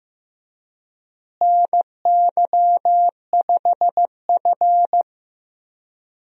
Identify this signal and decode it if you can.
Morse code, heard as a keyed tone, about 14 wpm, 700 Hz. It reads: NY5F
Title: USA Call Signs - 4 Characters - Repeated 3x - 15wpm